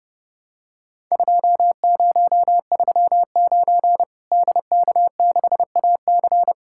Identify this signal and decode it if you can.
Morse code, heard as a keyed tone, about 30 wpm, 700 Hz. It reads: 2039 DK6AC
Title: Worked All Europe DX Contest - QTC Exchange - 30wpm